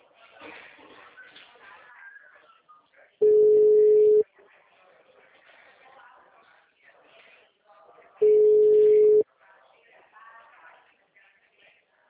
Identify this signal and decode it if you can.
Special information tone, with speech.
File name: out- -2003- - - .19392